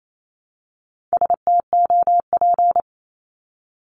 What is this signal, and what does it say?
Morse code, heard as a keyed tone, about 28 wpm, 700 Hz. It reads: STOP
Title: Top 500 Words 28wpm